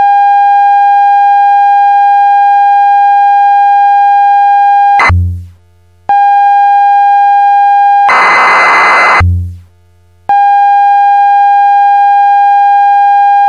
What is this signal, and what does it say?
ZX Spectrum tape loading.